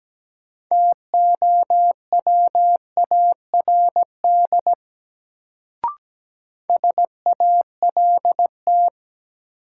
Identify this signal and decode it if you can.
Morse code, heard as a keyed tone, about 17 wpm, 700 Hz. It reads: TOWARD SALT
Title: Sets of 5 Words - New Word Drill - 17wpm